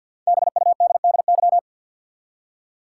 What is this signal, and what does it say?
Morse code, heard as a keyed tone, about 52 wpm, 700 Hz. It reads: BUDDY
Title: Five Letter Words - 50wpm